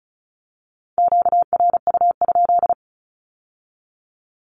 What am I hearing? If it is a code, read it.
Morse code, heard as a keyed tone, about 36 wpm, 700 Hz. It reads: QRU?
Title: Q-Codes with Spoken Meaning - 35wpm